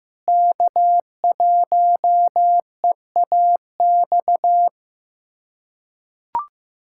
Morse code, heard as a keyed tone, about 15 wpm, 700 Hz. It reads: K1EAX
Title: USA Callsign DE Callsign 15wpm